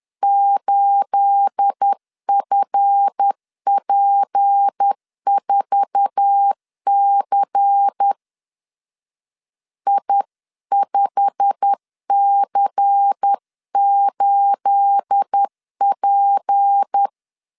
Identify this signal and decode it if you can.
Morse code, heard as a keyed tone, about 10 wpm, 800 Hz. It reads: 8FP4C I5C8P
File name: LEZIONI / 10 SIGNS / 10B INT 10wpm 2844